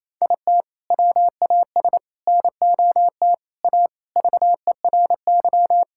Morse code, heard as a keyed tone, about 28 wpm, 700 Hz. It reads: IT WAS NOT A VERY
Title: Sentences from Top 1000 Words - Review Part 6 - 28wpm